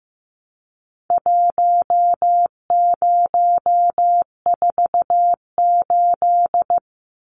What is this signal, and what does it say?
Morse code, heard as a keyed tone, about 15 wpm, 700 Hz. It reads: 1048
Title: Numbers - Contest Style - 4-Digit 15wpm